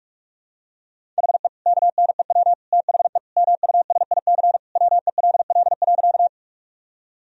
Morse code, heard as a keyed tone, about 46 wpm, 700 Hz. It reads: HE KNEW THE MUSIC WELL.
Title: Sentences from Top 1000 Words - Review Part 1 - 45wpm